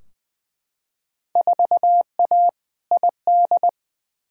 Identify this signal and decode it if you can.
Morse code, heard as a keyed tone, about 20 wpm, 700 Hz. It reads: 4A ID
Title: Field Day Exchanges Encore - 20wpm